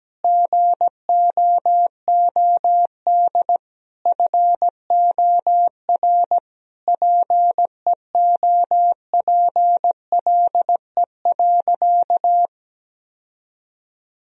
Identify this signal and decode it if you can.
Morse code, heard as a keyed tone, about 17 wpm, 700 Hz. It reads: GOOD FOR PEOPLE.
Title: Sentences from Top 1000 Words - Review Part 3 - 17wpm